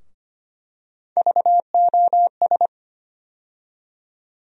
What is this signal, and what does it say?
Morse code, heard as a keyed tone, about 25 wpm, 700 Hz. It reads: VOS
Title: USA Call Sign Postfixes - 3 Letters 25wpm